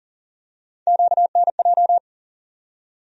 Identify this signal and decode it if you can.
Morse code, heard as a keyed tone, about 40 wpm, 700 Hz. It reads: QNJ
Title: USA Call Sign Postfixes - 3 Letters 40wpm